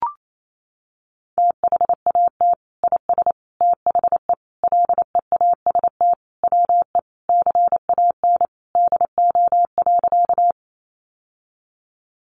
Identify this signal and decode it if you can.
Morse code, heard as a keyed tone, about 28 wpm, 700 Hz. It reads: THAT IS THE LEAST WE CAN DO.
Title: Sentences from Top 700 Words 28wpm